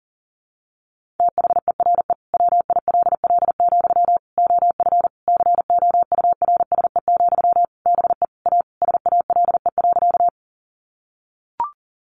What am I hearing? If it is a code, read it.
Morse code, heard as a keyed tone, about 40 wpm, 700 Hz. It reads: THERE WILL, OF COURSE, BE A SALE.
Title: Sentences from Top 1400 Words 40wpm